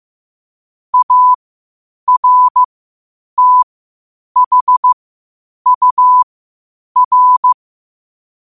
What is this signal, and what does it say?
Morse code, heard as a keyed tone, about 15 wpm, spaced as for 8 wpm, 1000 Hz. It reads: ARTHUR
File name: qotdfiles / 8wpm